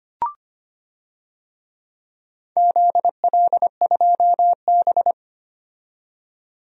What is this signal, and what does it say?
Morse code, heard as a keyed tone, about 25 wpm, 700 Hz. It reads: ZL2B
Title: DXCC-100 Call Signs - 4 Characters - 25wpm